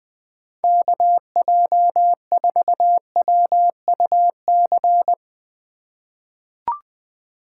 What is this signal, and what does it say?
Morse code, heard as a keyed tone, about 20 wpm, 700 Hz. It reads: KJ4WUC
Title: USA Call Signs - 6 Characters - 2x Character Spacing - 20wpm